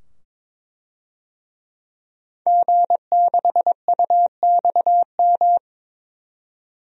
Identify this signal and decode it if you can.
Morse code, heard as a keyed tone, about 22 wpm, 700 Hz. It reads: G6UXM
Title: DXCC-100 Call Signs - 5 Characters - 22wpm